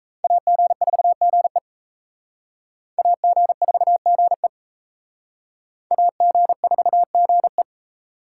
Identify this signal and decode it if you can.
Morse code, heard as a keyed tone, about 35 wpm, 700 Hz. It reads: AG4GE AG4GE AG4GE
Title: Speed Racing - USA 2x2 Call Signs 28wpm